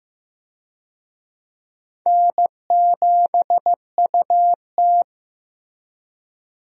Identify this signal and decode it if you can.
Morse code, heard as a keyed tone, about 15 wpm, 700 Hz. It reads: N7UT
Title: USA Call Signs - 4 Characters - Repeated 2x - 15wpm